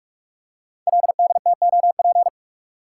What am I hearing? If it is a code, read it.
Morse code, heard as a keyed tone, about 44 wpm, 700 Hz. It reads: LDTOP
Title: Code Groups - 5 Letters - 45wpm